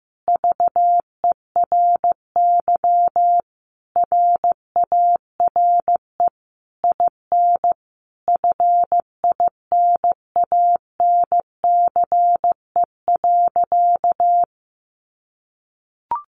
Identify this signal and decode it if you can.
Morse code, heard as a keyed tone, about 15 wpm, 700 Hz. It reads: VERY RARE IN FINANCE.
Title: Sentences from Top 1500 Words 15wpm